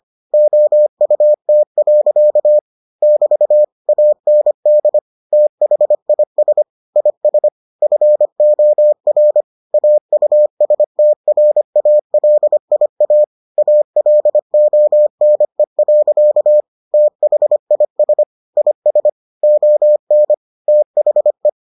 Morse code, heard as a keyed tone, about 25 wpm, 600 Hz. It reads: OUT. <BT> AND THIS IS FOR AUSTRALIA ALONE. THIS IS ON THE